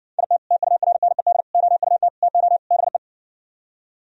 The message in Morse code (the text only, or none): IT TURNED OUT TO BE